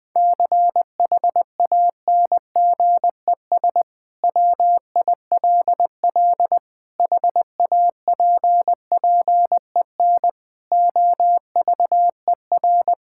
Morse code, heard as a keyed tone, about 20 wpm, 700 Hz. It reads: CHANGES WILL HAPPEN OVER
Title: Sentences with 5-Word Phrases - 20wpm